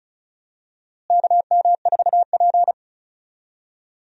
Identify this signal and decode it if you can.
Morse code, heard as a keyed tone, about 36 wpm, 700 Hz. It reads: KM4P
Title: USA Call Signs - 4 Characters - Repeated 3x - 35wpm